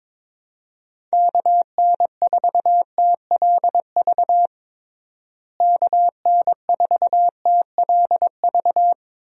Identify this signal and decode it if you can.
Morse code, heard as a keyed tone, about 22 wpm, 700 Hz. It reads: KN4TLV KN4TLV
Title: USA Call Signs - 6 Characters - Repeated 2x - 22wpm